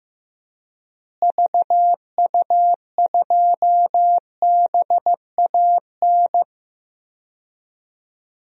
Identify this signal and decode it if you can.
Morse code, heard as a keyed tone, about 15 wpm, 700 Hz. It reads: VU2BAN
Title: DXCC-100 Call Signs - 6 Characters - Repeated 3x - 15wpm